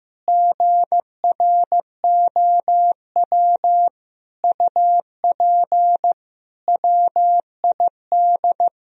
Morse code, heard as a keyed tone, about 15 wpm, 700 Hz. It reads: GROW UP WID
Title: Sentences from Top 300 Words and Abbreviations 15wpm